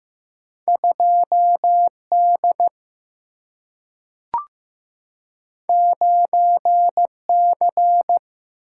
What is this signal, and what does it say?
Morse code, heard as a keyed tone, about 15 wpm, 700 Hz. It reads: 2 D 9 C
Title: Single-Letter Single-Number 15wpm